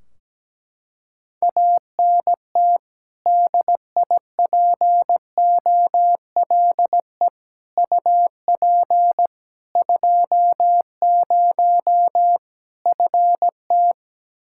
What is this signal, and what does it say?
Morse code, heard as a keyed tone, about 17 wpm, 700 Hz. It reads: ANT DIPOLE UP 20 FT
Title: Practice QSO - Exchange 2 - Info - 17wpm